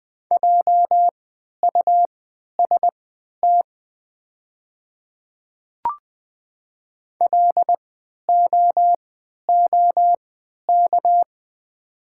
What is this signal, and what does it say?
Morse code, heard as a keyed tone, about 20 wpm, spaced as for 11 wpm, 700 Hz. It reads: JUST LOOK
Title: 3x Character Spacing - Top 200 Words - 20wpm